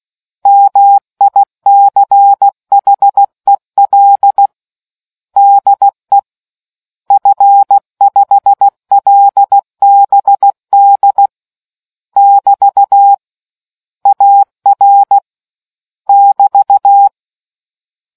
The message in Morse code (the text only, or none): MICHEL DE F5LBD <BT> AR <BT>